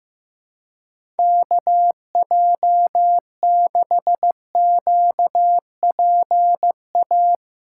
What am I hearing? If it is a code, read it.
Morse code, heard as a keyed tone, about 15 wpm, 700 Hz. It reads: KJ6QPA
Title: USA Callsign DE Callsign - 6 Characters - 15wpm